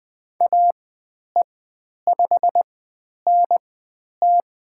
Morse code, heard as a keyed tone, about 20 wpm, spaced as for 10 wpm, 700 Hz. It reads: AE5NT
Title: 095 Lesson 16g - Letter c - Call Signs 20wpm